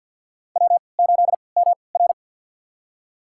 Morse code, heard as a keyed tone, about 50 wpm, spaced as for 28 wpm, 700 Hz. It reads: W8MR
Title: USA Call Signs - 4 Characters - 3x Character Spacing - 50wpm